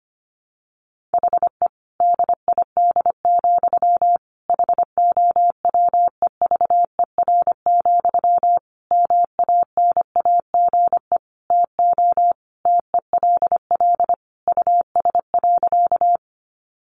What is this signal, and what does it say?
Morse code, heard as a keyed tone, about 25 wpm, 700 Hz. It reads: HE DID, HOWEVER, MANAGE TO TELL US.
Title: Sentences from Top 1400 Words 25wpm